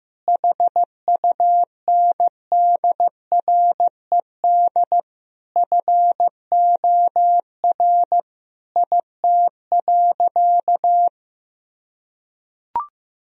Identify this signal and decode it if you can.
Morse code, heard as a keyed tone, about 16 wpm, 700 Hz. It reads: HUNDRED FOR IT.
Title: Sentences from Top 1000 Words - Review Part 2 - 15wpm